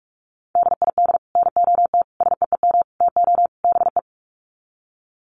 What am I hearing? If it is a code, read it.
Morse code, heard as a keyed tone, about 44 wpm, 700 Hz. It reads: DID NOT SEEM TO BE
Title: Sets of 5 Words Encore 45wpm